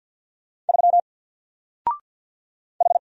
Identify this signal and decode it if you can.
Morse code, heard as a keyed tone, about 50 wpm, 700 Hz. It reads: 3 H
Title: Single Letter-Number 50wpm